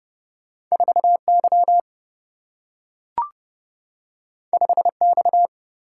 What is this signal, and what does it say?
Morse code, heard as a keyed tone, about 30 wpm, 700 Hz. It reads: 4Y 5X
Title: Single-Letter Single-Number 30wpm